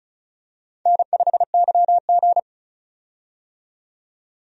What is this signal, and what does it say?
Morse code, heard as a keyed tone, about 34 wpm, 700 Hz. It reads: N5YG
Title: USA Call Signs - 4 Characters - 35wpm